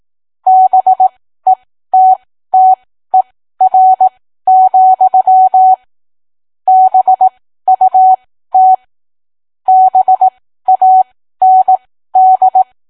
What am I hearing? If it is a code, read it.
Morse code, heard as a keyed tone, about 18 wpm, spaced as for 13 wpm, 750 Hz. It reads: BETTER, BUT BAND